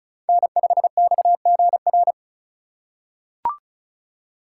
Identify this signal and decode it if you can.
Morse code, heard as a keyed tone, about 36 wpm, 700 Hz. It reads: N5XGR
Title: USA Call Signs - 5 Characters - 4x Character Spacing - 35wpm